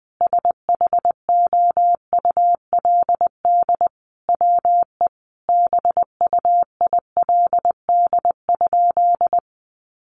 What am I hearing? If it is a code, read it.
Morse code, heard as a keyed tone, about 20 wpm, 700 Hz. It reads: SHOULD WE BUILD?
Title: Sentences from Top 1300 Words 20wpm